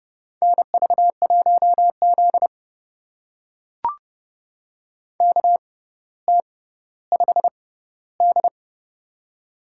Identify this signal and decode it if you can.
Morse code, heard as a keyed tone, about 30 wpm, spaced as for 15 wpm, 700 Hz. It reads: NV1Z KT5D
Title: USA Call Signs - 4 Characters - 6x Character Spacing - 30wpm